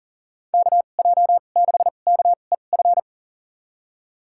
Morse code, heard as a keyed tone, about 40 wpm, spaced as for 29 wpm, 700 Hz. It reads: KJ6KEF
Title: USA Call Signs - 6 Characters - 2x Character Spacing - 40wpm